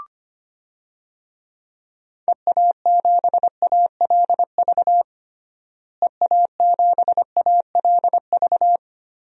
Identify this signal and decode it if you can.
Morse code, heard as a keyed tone, about 25 wpm, 700 Hz. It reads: EA7ALV EA7ALV
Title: DXCC-100 Call Signs - 6 Characters - Repeated 3x - 25wpm